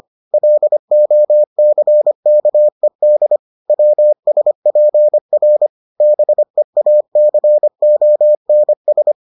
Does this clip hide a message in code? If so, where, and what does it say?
Yes: Morse code from 0.3 to 9.1 s, heard as a keyed tone, about 25 wpm, 600 Hz. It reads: LOCKED WSPR BEACONS